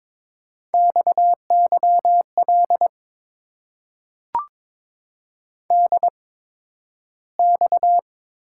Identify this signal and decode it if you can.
Morse code, heard as a keyed tone, about 22 wpm, spaced as for 12 wpm, 700 Hz. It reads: XYL DX
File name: Top 90 QSO Elements - 8x Character Spacing - 22wpm